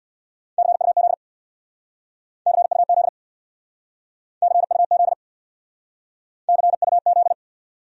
Morse code, heard as a keyed tone, about 64 wpm, 700 Hz. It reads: CUZ CUZ CUZ CUZ